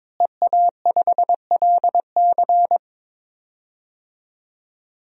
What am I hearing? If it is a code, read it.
Morse code, heard as a keyed tone, about 22 wpm, 700 Hz. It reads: EA5LC
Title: DXCC-100 Call Signs - 5 Characters - Repeated 2x - 22wpm